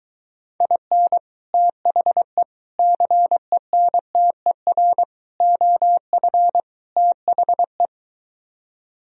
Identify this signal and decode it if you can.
Morse code, heard as a keyed tone, about 23 wpm, 700 Hz. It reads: IN THE CENTER OF THE